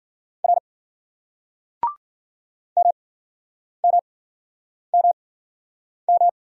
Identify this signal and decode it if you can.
Morse code, heard as a keyed tone, about 53 wpm, spaced as for 9 wpm, 700 Hz. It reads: R MMMM